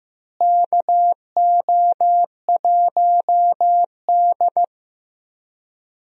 Morse code, heard as a keyed tone, about 15 wpm, 700 Hz. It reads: KO1D
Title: USA Call Signs - 4 Characters - Repeated 3x - 15wpm